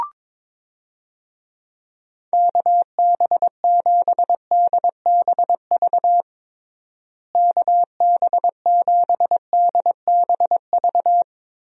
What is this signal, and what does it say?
Morse code, heard as a keyed tone, about 22 wpm, 700 Hz. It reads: KB7DBV KB7DBV
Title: USA Call Signs - 6 Characters - Repeated 2x - 22wpm